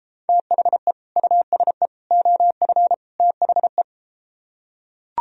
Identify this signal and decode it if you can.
Morse code, heard as a keyed tone, about 33 wpm, 700 Hz. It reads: THE USE OF THE